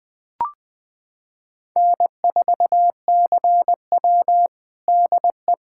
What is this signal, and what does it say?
Morse code, heard as a keyed tone, about 20 wpm, 700 Hz. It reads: N4CW DE
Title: Practice QSO - Exchange 4 - Ending - 20wpm